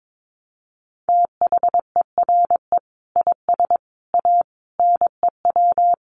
Morse code, heard as a keyed tone, about 22 wpm, 700 Hz. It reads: THERE IS A NEW
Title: Sentences from Top 1100 Words 22wpm